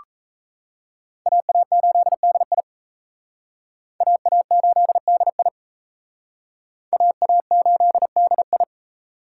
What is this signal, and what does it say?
Morse code, heard as a keyed tone, about 37 wpm, 700 Hz. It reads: AA8DI AA8DI AA8DI